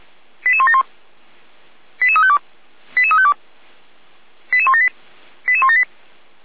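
Five-tone selective call (ZVEI), heard as quick runs of five tones, tones of about 70 ms. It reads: 88171 88352 88352 88178 88178
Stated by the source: ZVEI five-tone selective call received on an SDR.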